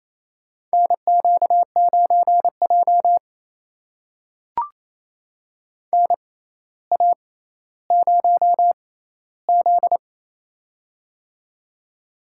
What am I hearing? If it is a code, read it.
Morse code, heard as a keyed tone, about 28 wpm, spaced as for 14 wpm, 700 Hz. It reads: NQ9J NA0Z
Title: USA Call Signs - 4 Characters - 6x Character Spacing - 28wpm